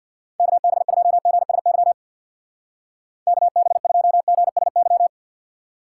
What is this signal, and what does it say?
Morse code, heard as a keyed tone, about 60 wpm, 700 Hz. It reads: KB2GSY KB2GSY
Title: Speed Racing - USA 2x3 Call Signs 50wpm